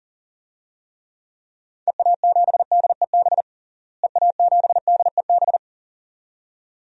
Morse code, heard as a keyed tone, about 40 wpm, 700 Hz. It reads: EA7DEB EA7DEB
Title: DXCC-100 Call Signs - 6 Characters - Repeated 2x - 40wpm